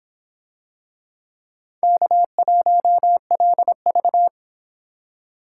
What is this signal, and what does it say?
Morse code, heard as a keyed tone, about 23 wpm, 700 Hz. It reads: K1LV